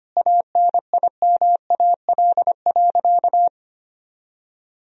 Morse code, heard as a keyed tone, about 25 wpm, 700 Hz. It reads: ANIMAL.
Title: Sentences from Top 1000 Words - Review Part 1 - 25wpm